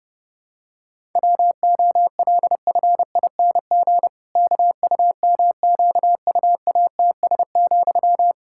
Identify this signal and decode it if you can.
Morse code, heard as a keyed tone, about 30 wpm, 700 Hz. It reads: WOLFING KUMQUATS,
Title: Pangram Challenge v2 - 30wpm